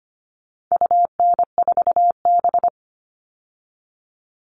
Morse code, heard as a keyed tone, about 25 wpm, 700 Hz. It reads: UN4B
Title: All Cut Numbers - 4-Digit 25wpm